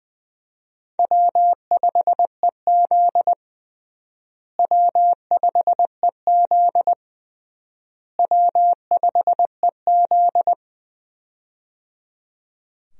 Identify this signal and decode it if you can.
Morse code, heard as a keyed tone, about 20 wpm, 700 Hz. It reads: W5EZ W5EZ W5EZ
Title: USA Call Signs - 4 Characters - Repeated 3x - 20wpm